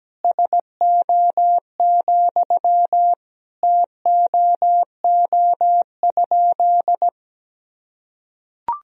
Morse code, heard as a keyed tone, about 17 wpm, 700 Hz. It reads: SO, TOO?
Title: Sentences from Top 200 Words and Abbreviations 17wpm